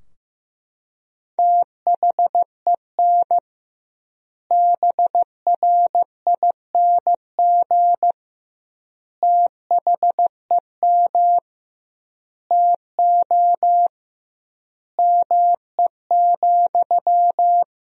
Morse code, heard as a keyed tone, about 15 wpm, 700 Hz. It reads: THEN BRING THEM TO ME,
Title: 2x Word-Spacing - Sentences from Top 500 Words - 15wpm